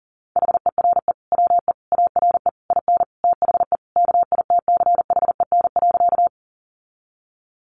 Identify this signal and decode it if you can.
Morse code, heard as a keyed tone, about 40 wpm, 700 Hz. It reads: HERE WE ARE IN THE KITCHEN.
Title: Sentences from Top 1400 Words 40wpm